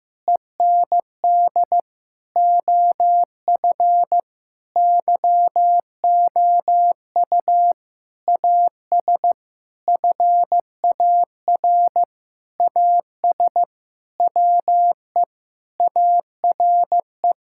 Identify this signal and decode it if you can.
Morse code, heard as a keyed tone, about 15 wpm, 700 Hz. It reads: END OF YOU AS FAR AS WE ARE
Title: Sentences from Top 1000 Words - Review Part 6 - 15wpm